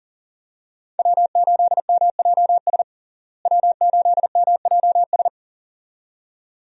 Morse code, heard as a keyed tone, about 37 wpm, 700 Hz. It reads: W8MJS W8MJS
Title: USA Call Signs - 5 Characters - Repeated 2x - 40wpm